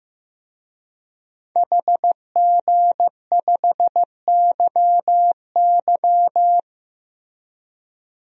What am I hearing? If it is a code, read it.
Morse code, heard as a keyed tone, about 15 wpm, 700 Hz. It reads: HG5YY
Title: DXCC-100 Call Signs - 5 Characters - Repeated 3x - 15wpm